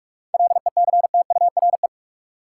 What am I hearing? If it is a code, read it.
Morse code, heard as a keyed tone, about 46 wpm, 700 Hz. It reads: LECTURE
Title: Sentences from Top 1600 Words - New Word Drill - 45wpm